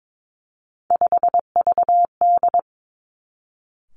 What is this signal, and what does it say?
Morse code, heard as a keyed tone, about 22 wpm, 700 Hz. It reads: 5VD
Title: Three Challenging Letter-Number Combinations 22wpm